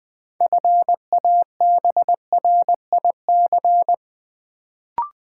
Morse code, heard as a keyed tone, about 20 wpm, 700 Hz. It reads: FABRIC
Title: Build A Word - 3 Letters At A Time - 6-Letter Words - 20wpm